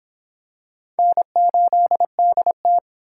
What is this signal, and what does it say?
Morse code, heard as a keyed tone, about 23 wpm, 700 Hz. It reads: N8DT